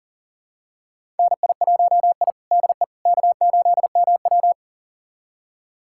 Morse code, heard as a keyed tone, about 40 wpm, 700 Hz. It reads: NI1I DE K8MW
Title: USA Callsign DE Callsign - 4 Characters - 40wpm